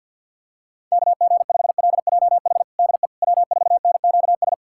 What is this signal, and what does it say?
Morse code, heard as a keyed tone, about 52 wpm, 700 Hz. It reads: KG5LJH DE W4NQS
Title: USA Callsign DE Callsign 50wpm